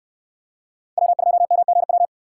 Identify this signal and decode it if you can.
Morse code, heard as a keyed tone, about 68 wpm, 700 Hz. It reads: W2MGW